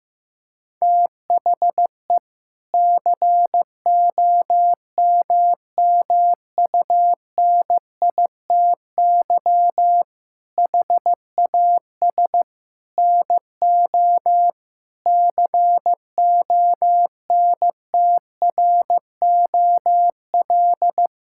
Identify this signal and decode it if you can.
Morse code, heard as a keyed tone, about 15 wpm, 700 Hz. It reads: THE COMMUNITY HAS NO CONTROL
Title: Sentences from Top 1000 Words - Review Part 5 - 15wpm